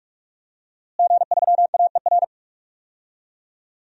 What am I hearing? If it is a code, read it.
Morse code, heard as a keyed tone, about 46 wpm, 700 Hz. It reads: G3AER
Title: DXCC-100 Call Signs - 5 Characters - Repeated 3x - 45wpm